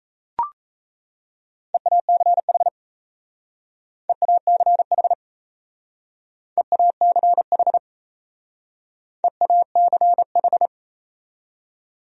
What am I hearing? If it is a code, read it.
Morse code, heard as a keyed tone, about 37 wpm, 700 Hz. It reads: EACH EACH EACH EACH